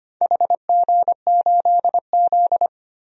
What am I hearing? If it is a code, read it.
Morse code, heard as a keyed tone, about 25 wpm, 700 Hz. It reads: HG8Z
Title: DXCC-100 Call Signs - 4 Characters - Repeated 3x - 25wpm